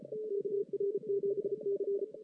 Morse code, heard as a keyed tone, about 34 wpm, 400 Hz. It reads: KR7C
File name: 2016 JAN CW NAQP